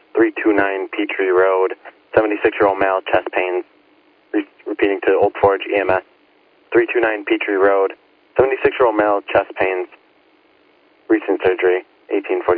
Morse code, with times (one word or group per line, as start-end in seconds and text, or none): none